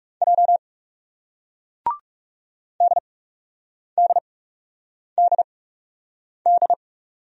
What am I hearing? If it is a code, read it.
Morse code, heard as a keyed tone, about 38 wpm, 700 Hz. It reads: J D D D D